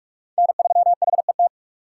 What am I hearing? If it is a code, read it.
Morse code, heard as a keyed tone, about 48 wpm, 700 Hz. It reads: N3HET
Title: USA Call Signs - 5 Characters - 6x Character Spacing - 45wpm